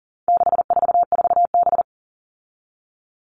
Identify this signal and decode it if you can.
Morse code, heard as a keyed tone, about 38 wpm, 700 Hz. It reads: 644B
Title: All Cut Numbers - 4-Digit 40wpm